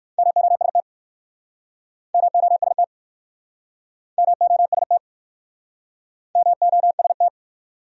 Morse code, heard as a keyed tone, about 63 wpm, 700 Hz. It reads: MOST MOST MOST MOST